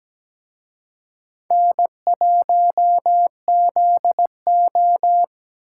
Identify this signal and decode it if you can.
Morse code, heard as a keyed tone, about 20 wpm, 700 Hz. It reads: N1ZO
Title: Speed Racing - USA 1x2 Call Signs 17wpm